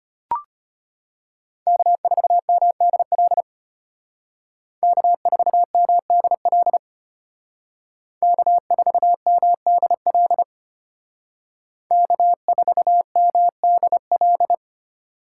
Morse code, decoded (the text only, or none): K4MDL K4MDL K4MDL K4MDL